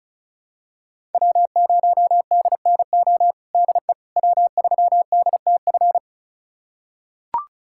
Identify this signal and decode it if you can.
Morse code, heard as a keyed tone, about 36 wpm, 700 Hz. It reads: W0DNO DE W3DTF
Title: USA Callsign DE Callsign - 5 Characters - 35wpm